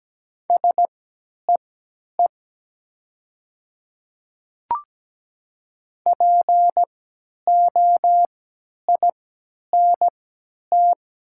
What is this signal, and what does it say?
Morse code, heard as a keyed tone, about 17 wpm, spaced as for 10 wpm, 700 Hz. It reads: SEE POINT
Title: 3x Character Spacing - Top 500 Words - 17wpm